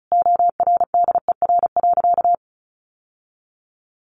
Morse code, heard as a keyed tone, about 34 wpm, 700 Hz. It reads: ORDER.
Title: Sentences from Top 900 Words 35wpm